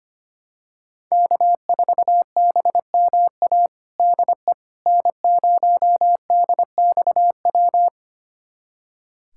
Morse code, heard as a keyed tone, about 25 wpm, 700 Hz. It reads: K4BMA DE N0DXW
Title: USA Callsign DE Callsign - 5 Characters - 25wpm